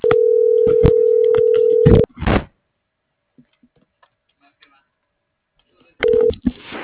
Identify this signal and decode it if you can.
Ringback tone (North America), 440 + 480 Hz.